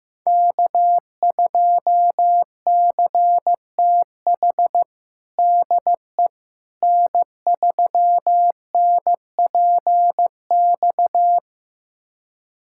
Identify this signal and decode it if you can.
Morse code, heard as a keyed tone, about 15 wpm, 700 Hz. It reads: K2CTH DE N3NPX
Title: USA Callsign DE Callsign - 5 Characters - 15wpm